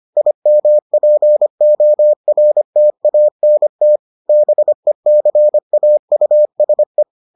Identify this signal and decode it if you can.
Morse code, heard as a keyed tone, about 25 wpm, 600 Hz. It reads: IMPORTANT BECAUSE